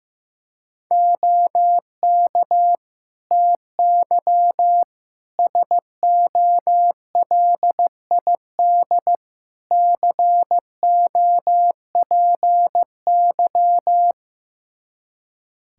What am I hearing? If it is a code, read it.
Morse code, heard as a keyed tone, about 15 wpm, 700 Hz. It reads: OK TY SOLID COPY
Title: K1USN SST QSO Practice 15wpm